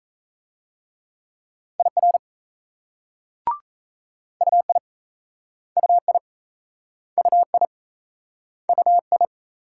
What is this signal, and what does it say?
Morse code, heard as a keyed tone, about 36 wpm, 700 Hz. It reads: IR UI UI UI UI